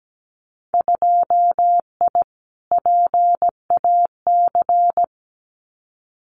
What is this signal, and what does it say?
Morse code, heard as a keyed tone, about 17 wpm, 700 Hz. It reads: 2I PAC
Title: Winter Field Day Exchanges Encore 17wpm